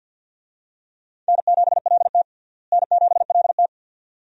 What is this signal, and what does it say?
Morse code, heard as a keyed tone, about 52 wpm, 700 Hz. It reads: N7LT N7LT
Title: USA Call Signs - 4 Characters - Repeated 2x - 50wpm